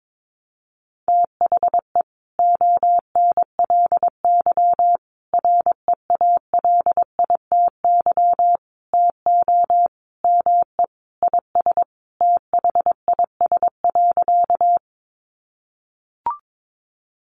Morse code, heard as a keyed tone, about 22 wpm, 700 Hz. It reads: THE ONLY REALITY TO ME IS THIS.
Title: Sentences from Top 1200 Words 22wpm